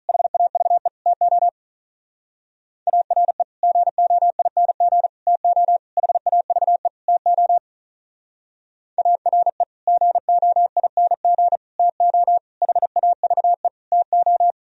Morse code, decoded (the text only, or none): HAVE TO ARE GOING TO HAVE TO ARE GOING TO HAVE TO